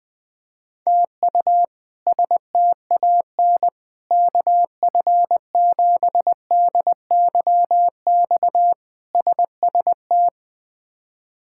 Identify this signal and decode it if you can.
Morse code, heard as a keyed tone, about 21 wpm, 700 Hz. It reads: TU STAN KF7DYX SST
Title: K1USN SST QSO Practice - 20wpm